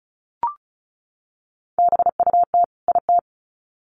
Morse code, heard as a keyed tone, about 34 wpm, 700 Hz. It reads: BUT IT
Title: Sets of 2 Words 35wpm